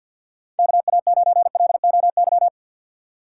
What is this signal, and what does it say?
Morse code, heard as a keyed tone, about 49 wpm, 700 Hz. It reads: KA9LOY
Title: USA Call Signs - 6 Characters - 4x Character Spacing - 50wpm